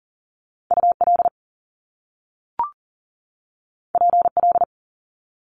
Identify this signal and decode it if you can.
Morse code, heard as a keyed tone, about 39 wpm, 700 Hz. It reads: UL PL
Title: Top Two Letter Combinations in Words 40wpm